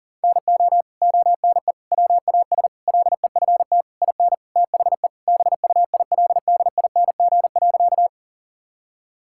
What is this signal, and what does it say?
Morse code, heard as a keyed tone, about 40 wpm, 700 Hz. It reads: NO ONE WAS LEFT IN THE BUILDING.
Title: Sentences from Top 1000 Words - Review Part 5 - 40wpm